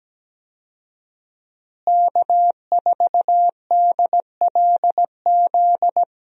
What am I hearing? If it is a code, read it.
Morse code, heard as a keyed tone, about 17 wpm, 700 Hz. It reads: K4DLZ
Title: USA Call Signs - 5 Characters - Repeated 3x - 17wpm